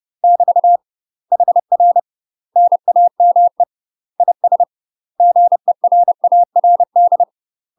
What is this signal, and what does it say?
Morse code, heard as a keyed tone, about 30 wpm, 700 Hz. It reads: <BT> HR NAME IS GERARD
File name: UK MASTER QSOs TRAINING / 30WPM / UK MASTER QSOs 0035